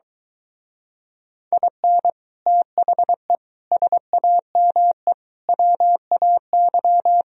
Morse code, heard as a keyed tone, about 22 wpm, 700 Hz. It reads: IN THE SAME WAY